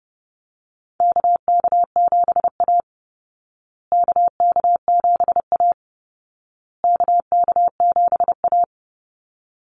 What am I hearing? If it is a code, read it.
Morse code, heard as a keyed tone, about 30 wpm, 700 Hz. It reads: KK7A KK7A KK7A
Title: Call Signs Repeated 2x - 30wpm